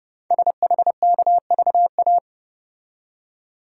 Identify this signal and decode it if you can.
Morse code, heard as a keyed tone, about 30 wpm, 700 Hz. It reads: SHKVA
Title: Code Groups - 5 Letters - 30wpm